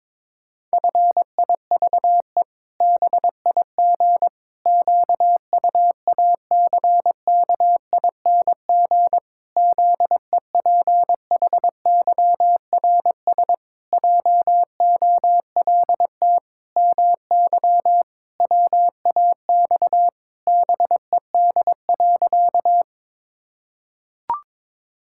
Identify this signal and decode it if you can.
Morse code, heard as a keyed tone, about 22 wpm, 700 Hz. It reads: FIVE BIG QUACKING ZEPHYRS JOLT MY WAX BED.
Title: Pangram Challenge v2 - 22wpm